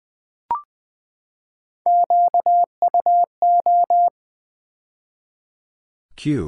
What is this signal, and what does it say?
Morse code, heard as a keyed tone, about 20 wpm, 700 Hz. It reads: QUO
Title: Build A Word - 3 Letters At A Time - 9-Letter Words - 20wpm